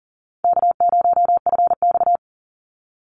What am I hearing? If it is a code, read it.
Morse code, heard as a keyed tone, about 40 wpm, 700 Hz. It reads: K0FX
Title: Easy Call Signs Repeated 3x - 40wpm